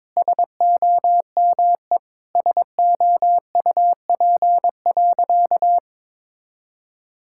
Morse code, heard as a keyed tone, about 22 wpm, 700 Hz. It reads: SOME SOUP.
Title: Sentences from Top 1000 Words - Review Part 6 - 22wpm